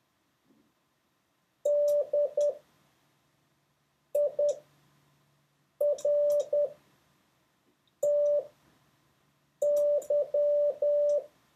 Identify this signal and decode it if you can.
Morse code, heard as a keyed tone, about 10 wpm, spaced as for 5 wpm, 580 Hz. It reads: DIRTY